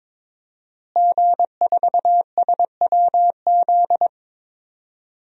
Morse code, heard as a keyed tone, about 22 wpm, 700 Hz. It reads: G4SWZ
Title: DXCC-100 Call Signs - 5 Characters - Repeated 3x - 22wpm